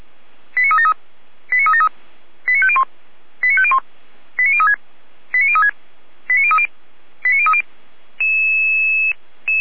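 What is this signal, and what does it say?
Five-tone selective call (ZVEI), heard as quick runs of five tones, tones of about 80 ms. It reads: 89373 89373 89501 89501 89036 89036 89030 89030
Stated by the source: ZVEI five-tone call 89010 received by SDR, recorded by fire brigade Feuerwehr Orlamünde.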